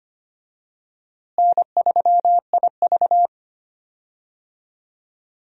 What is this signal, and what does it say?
Morse code, heard as a keyed tone, about 25 wpm, 700 Hz. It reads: N3IV
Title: USA Call Signs - 4 Characters - Repeated 2x - 25wpm